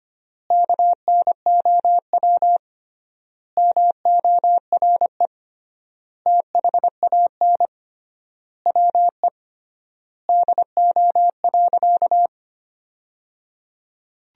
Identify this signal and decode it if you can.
Morse code, heard as a keyed tone, about 25 wpm, 700 Hz. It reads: KNOW MORE THAN WE DO.
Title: 3x Word-Spacing - Sentences from Top 100 Words - 25wpm